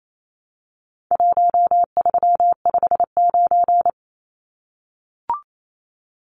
Morse code, heard as a keyed tone, about 28 wpm, 700 Hz. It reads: 1359
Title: Numbers - Contest Style - 4-Digit 28wpm